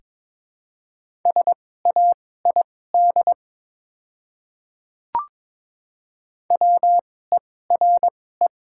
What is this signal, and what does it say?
Morse code, heard as a keyed tone, about 22 wpm, spaced as for 16 wpm, 700 Hz. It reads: SAID WERE
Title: 2x Character Spacing - Top 100 Words - 22wpm